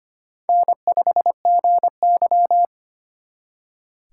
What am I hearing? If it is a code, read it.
Morse code, heard as a keyed tone, about 31 wpm, 700 Hz. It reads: N5GY
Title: Speed Racing - USA 1x2 Call Signs 25wpm